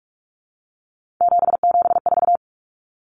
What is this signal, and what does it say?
Morse code, heard as a keyed tone, about 44 wpm, 700 Hz. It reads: 774
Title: Common Cut Numbers - 3-Digit 45wpm